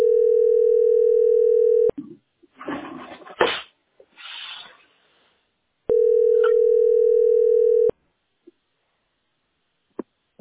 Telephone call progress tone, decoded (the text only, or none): ringback tone